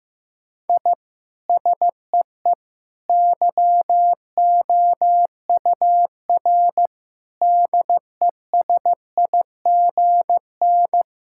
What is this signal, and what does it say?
Morse code, heard as a keyed tone, about 16 wpm, 700 Hz. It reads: I SEE YOUR DESIGN
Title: Sentences from Top 1000 Words - Review Part 5 - 15wpm